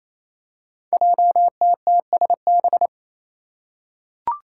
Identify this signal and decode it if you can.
Morse code, heard as a keyed tone, about 28 wpm, 700 Hz. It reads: JTTSB
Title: Five Challenging Letter-Number Combinations 28wpm